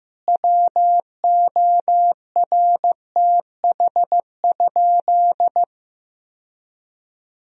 Morse code, heard as a keyed tone, about 15 wpm, 700 Hz. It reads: WORTH?
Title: Sentences from Top 1000 Words - Review Part 2 - 15wpm